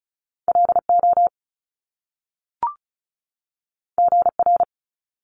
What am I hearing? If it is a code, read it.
Morse code, heard as a keyed tone, about 35 wpm, 700 Hz. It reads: LO GR
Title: Top Two Letter Combinations in Words 35wpm